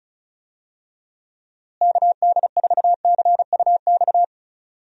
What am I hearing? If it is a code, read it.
Morse code, heard as a keyed tone, about 34 wpm, 700 Hz. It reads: KD4CUX
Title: USA Call Signs - 6 Characters - Repeated 2x - 35wpm